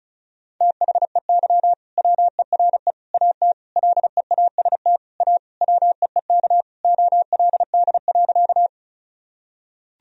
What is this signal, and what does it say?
Morse code, heard as a keyed tone, about 34 wpm, 700 Hz. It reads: THEY WERE AT LEAST A WEEK OLD.
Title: Sentences from Top 1000 Words - Review Part 2 - 35wpm